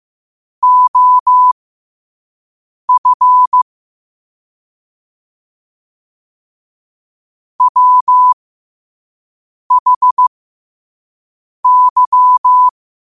Morse code, heard as a keyed tone, about 15 wpm, spaced as for 5 wpm, 1000 Hz. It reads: OF WHY